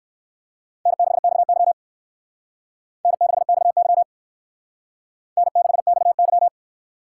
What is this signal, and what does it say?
Morse code, heard as a keyed tone, about 59 wpm, 700 Hz. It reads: N6XY N6XY N6XY